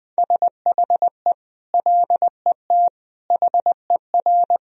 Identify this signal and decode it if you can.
Morse code, heard as a keyed tone, about 21 wpm, 700 Hz. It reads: SHE LET HER
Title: Sentences from Top 1000 Words - Review Part 1 - 20wpm